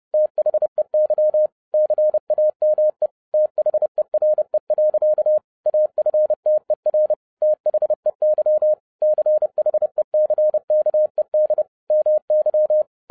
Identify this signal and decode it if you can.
Morse code, heard as a keyed tone, about 30 wpm, 610 Hz. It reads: THEY CAME THERE. AFTER THEY CHECKED MY
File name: 2011-07-mp3 / n55 BACK FROM A BIG ACCIDENT 30wpm